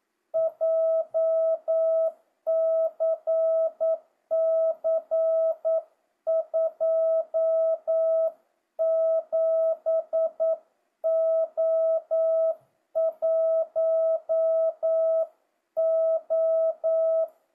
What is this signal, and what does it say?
Morse code, heard as a keyed tone, about 9 wpm, 650 Hz. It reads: JCC 27O1O
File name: jh8cbh / 2025cw